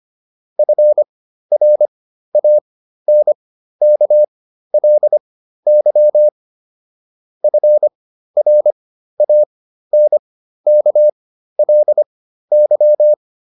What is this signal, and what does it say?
Morse code, heard as a keyed tone, about 25 wpm, spaced as for 13 wpm, 600 Hz. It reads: FRANKLY FRANKLY